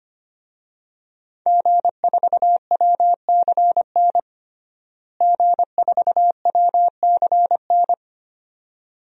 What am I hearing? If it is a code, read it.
Morse code, heard as a keyed tone, about 25 wpm, 700 Hz. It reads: G4WCN G4WCN
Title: DXCC-100 Call Signs - 5 Characters - Repeated 2x - 25wpm